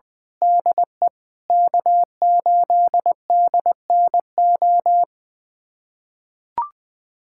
Morse code, heard as a keyed tone, about 20 wpm, 700 Hz. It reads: DE K8DNO
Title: USA Callsign DE Callsign 20wpm